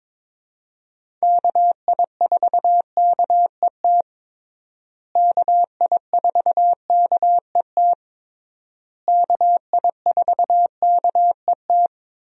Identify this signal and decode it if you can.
Morse code, heard as a keyed tone, about 22 wpm, 700 Hz. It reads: KI4KET KI4KET KI4KET
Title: USA Call Signs - 6 Characters - Repeated 3x - 22wpm